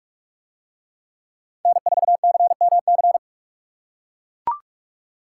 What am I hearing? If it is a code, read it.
Morse code, heard as a keyed tone, about 41 wpm, 700 Hz. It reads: N4CMC